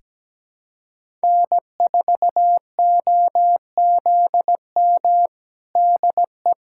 Morse code, heard as a keyed tone, about 17 wpm, 700 Hz. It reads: N4OZM DE
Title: USA Callsign DE Callsign - 5 Characters - 17wpm